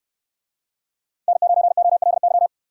Morse code, heard as a keyed tone, about 69 wpm, 700 Hz. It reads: N9OLY